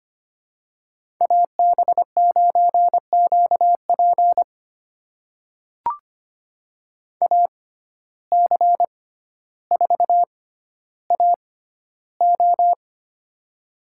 Morse code, heard as a keyed tone, about 25 wpm, spaced as for 13 wpm, 700 Hz. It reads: AB9QP AC4AO
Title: USA Call Signs - 5 Characters - 6x Character Spacing - 25wpm